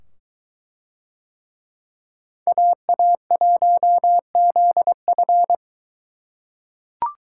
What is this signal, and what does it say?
Morse code, heard as a keyed tone, about 22 wpm, 700 Hz. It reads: AA1ZF